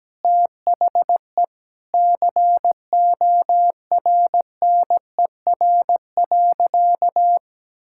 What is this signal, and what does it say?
Morse code, heard as a keyed tone, about 17 wpm, 700 Hz. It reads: THE CORNER.
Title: Sentences from Top 1600 Words 17wpm